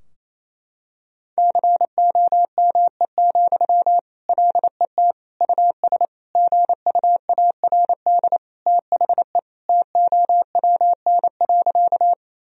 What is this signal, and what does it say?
Morse code, heard as a keyed tone, about 27 wpm, 700 Hz. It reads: COME, LET US GUARD THE TOWN.
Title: Sentences from Top 1000 Words - Review Part 4 - 28wpm